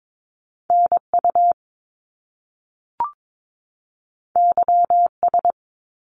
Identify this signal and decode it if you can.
Morse code, heard as a keyed tone, about 22 wpm, 700 Hz. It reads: NU YS
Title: Top Two Letter Combinations in Words 22wpm